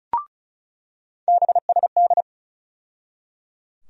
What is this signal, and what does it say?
Morse code, heard as a keyed tone, about 34 wpm, 700 Hz. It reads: BSD
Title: USA Call Sign Postfixes - 3 Letters 35wpm